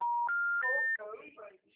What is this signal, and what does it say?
Special information tone, with speech.